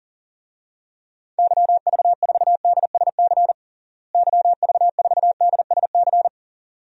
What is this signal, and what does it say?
Morse code, heard as a keyed tone, about 37 wpm, 700 Hz. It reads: YV4DSC YV4DSC
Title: DXCC-100 Call Signs - 6 Characters - Repeated 2x - 40wpm